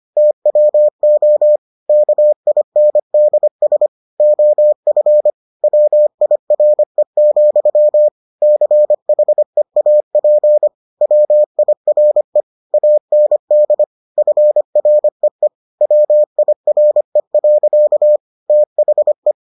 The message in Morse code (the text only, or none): TWO KINDS OF WIRE, CHEAP WIRE AND FREE WIRE. THE